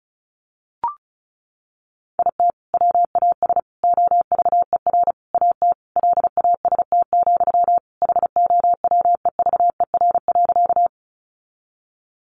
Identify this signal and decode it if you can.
Morse code, heard as a keyed tone, about 34 wpm, 700 Hz. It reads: IT WAS OVER AT LAST, HOWEVER.
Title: Sentences from Top 600 Words 35wpm